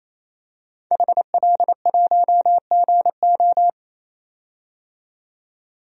Morse code, heard as a keyed tone, about 27 wpm, 700 Hz. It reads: HL1GO
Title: DXCC-100 Call Signs - 5 Characters - 28wpm